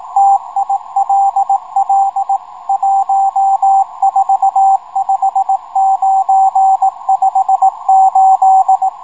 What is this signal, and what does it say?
Morse code, heard as a keyed tone, about 18 wpm, 780 Hz, with radio noise. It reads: TILL 145958
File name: rs-12 robot